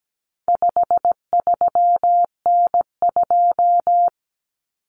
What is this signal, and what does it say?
Morse code, heard as a keyed tone, about 17 wpm, 700 Hz. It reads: 53N2
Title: Common Cut Numbers - 4-Digit 17wpm